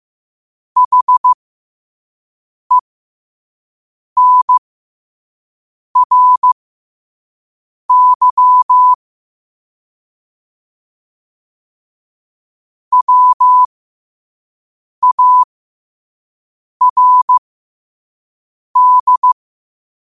Morse code, heard as a keyed tone, about 15 wpm, spaced as for 5 wpm, 1000 Hz. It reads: HENRY WARD
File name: qotdfiles / 5wpm